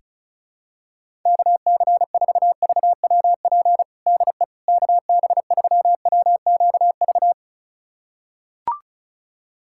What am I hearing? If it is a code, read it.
Morse code, heard as a keyed tone, about 36 wpm, 700 Hz. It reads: KC4VWP DE KB3WQV
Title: USA Callsign DE Callsign 35wpm